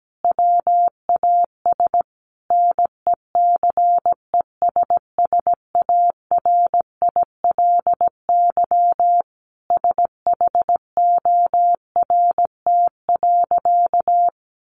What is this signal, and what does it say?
Morse code, heard as a keyed tone, about 17 wpm, 700 Hz. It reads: WAS NECESSARILY SHORT.
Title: Sentences from Top 1400 Words 17wpm